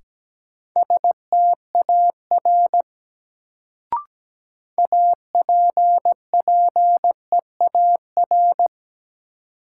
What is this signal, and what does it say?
Morse code, heard as a keyed tone, about 17 wpm, 700 Hz. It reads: STAR APPEAR
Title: Sentences from Top 1000 Words - Review Part 2 - New Word Drill - 17wpm